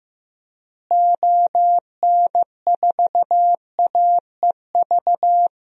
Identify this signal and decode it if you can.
Morse code, heard as a keyed tone, about 15 wpm, 700 Hz. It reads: ON4AEV
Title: DXCC-100 Call Signs - 6 Characters - Repeated 3x - 15wpm